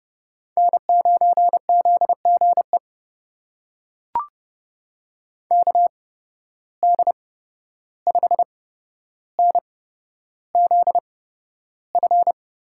Morse code, heard as a keyed tone, about 31 wpm, spaced as for 12 wpm, 700 Hz. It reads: N9ZGE KD5NZF
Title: USA Call Signs - 8x Character Spacing - 30wpm